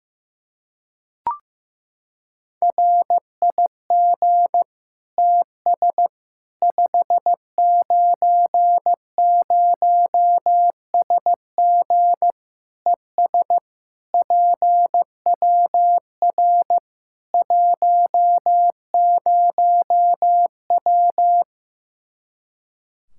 Morse code, heard as a keyed tone, about 15 wpm, 700 Hz. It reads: RIG TS 590SG ES PWR 10W
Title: Practice QSO - Exchange 2 - Info - 15wpm